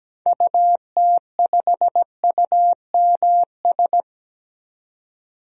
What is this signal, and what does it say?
Morse code, heard as a keyed tone, about 17 wpm, 700 Hz. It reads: UT5UMS
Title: DXCC-100 Call Signs - 6 Characters - 17wpm